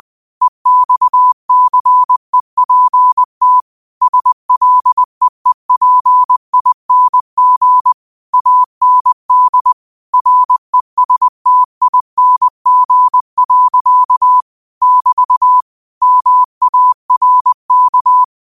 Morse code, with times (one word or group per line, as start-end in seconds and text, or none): 0.4-3.6 EXCEPT
4.0-7.9 SLEEPING
8.3-9.7 AND
10.1-14.4 RESTING.
14.8-15.6 <BT>
16.0-18.2 MARK